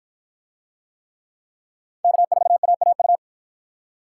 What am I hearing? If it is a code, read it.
Morse code, heard as a keyed tone, about 53 wpm, 700 Hz. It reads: K4AAU